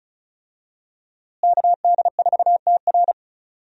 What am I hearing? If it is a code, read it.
Morse code, heard as a keyed tone, about 36 wpm, 700 Hz. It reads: KD4TR
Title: USA Call Signs - 5 Characters - Repeated 3x - 35wpm